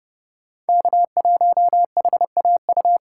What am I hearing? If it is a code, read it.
Morse code, heard as a keyed tone, about 32 wpm, 700 Hz. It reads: K1HAU
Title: USA Call Signs - 4x Character Spacing - 30wpm